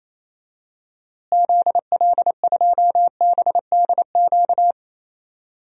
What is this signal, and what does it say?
Morse code, heard as a keyed tone, about 28 wpm, 700 Hz. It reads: ZL2BDQ
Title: DXCC-100 Call Signs - 6 Characters - Repeated 2x - 28wpm